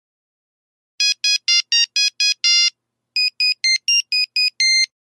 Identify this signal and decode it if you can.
There is no code.